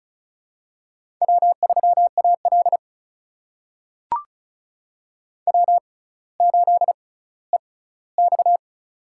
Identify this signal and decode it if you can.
Morse code, heard as a keyed tone, about 36 wpm, spaced as for 18 wpm, 700 Hz. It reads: W3AL W8EX
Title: USA Call Signs - 4 Characters - 6x Character Spacing - 35wpm